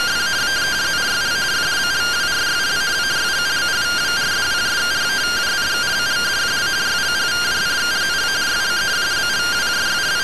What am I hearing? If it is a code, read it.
Two-tone teleprinter signal, about 170 Hz shift.